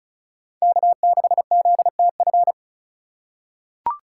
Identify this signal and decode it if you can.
Morse code, heard as a keyed tone, about 36 wpm, 700 Hz. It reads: K6ZTF
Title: USA Call Signs - 5 Characters - 2x Character Spacing - 35wpm